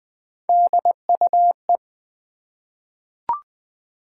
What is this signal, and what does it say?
Morse code, heard as a keyed tone, about 20 wpm, 700 Hz. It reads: DUE
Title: Three Letter Words - 20wpm